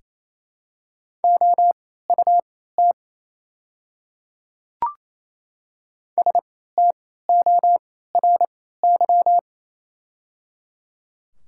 Morse code, heard as a keyed tone, about 28 wpm, spaced as for 16 wpm, 700 Hz. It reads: OUT STORY
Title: 3x Character Spacing - Top 300 Words - 28wpm